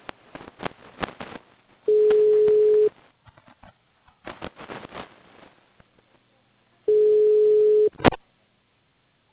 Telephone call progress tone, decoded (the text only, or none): ringback tone